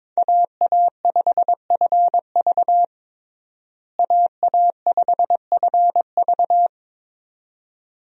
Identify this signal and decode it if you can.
Morse code, heard as a keyed tone, about 21 wpm, 700 Hz. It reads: AA5FV AA5FV
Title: USA Call Signs - 5 Characters - Repeated 2x - 22wpm